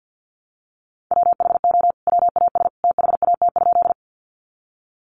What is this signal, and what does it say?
Morse code, heard as a keyed tone, about 50 wpm, 700 Hz. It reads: WHO WAS THAT?
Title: Sentences from Top 100 Words 50wpm